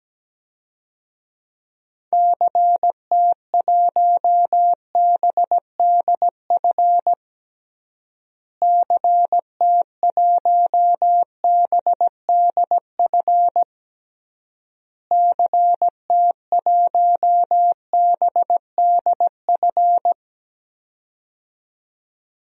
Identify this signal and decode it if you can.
Morse code, heard as a keyed tone, about 17 wpm, 700 Hz. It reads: CT1BDF CT1BDF CT1BDF
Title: DXCC-100 Call Signs - 6 Characters - Repeated 3x - 17wpm